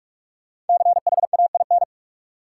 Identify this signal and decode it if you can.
Morse code, heard as a keyed tone, about 46 wpm, 700 Hz. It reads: CHAIN
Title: Sentences from Top 1500 Words - New Word Drill - 45wpm